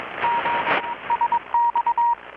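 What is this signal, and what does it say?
Morse code, heard as a keyed tone, about 21 wpm, 940 Hz, with radio noise. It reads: G SX
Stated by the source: Morse from KPH's Night of Nights XX maritime radiotelegraphy event, received in Chiapas, Mexico about 3,800 km away on a Kenwood R-600 with a grounded 10 m long-wire antenna.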